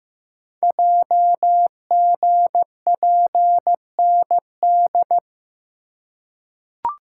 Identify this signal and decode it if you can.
Morse code, heard as a keyed tone, about 15 wpm, 700 Hz. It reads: JGPND
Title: Code Groups - 5 Letters - 15wpm